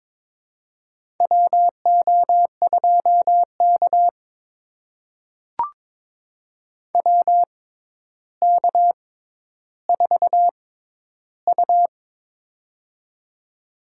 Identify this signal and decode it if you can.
Morse code, heard as a keyed tone, about 22 wpm, spaced as for 11 wpm, 700 Hz. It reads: WO2K WK4U
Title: USA Call Signs - 6x Character Spacing - 22wpm